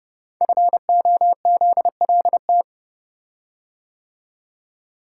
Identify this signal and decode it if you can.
Morse code, heard as a keyed tone, about 30 wpm, 700 Hz. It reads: FOZLT
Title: Code Groups - 5 Letters - 30wpm